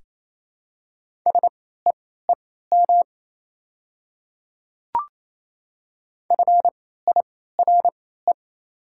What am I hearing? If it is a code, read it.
Morse code, heard as a keyed tone, about 28 wpm, spaced as for 16 wpm, 690 Hz. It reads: SEEM FIRE
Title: 3x Character Spacing - Top 500 Words - 28wpm